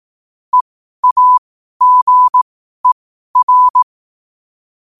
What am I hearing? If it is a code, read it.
Morse code, heard as a keyed tone, about 18 wpm, spaced as for 12 wpm, 1000 Hz. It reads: EAGER
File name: qotdfiles / 12wpm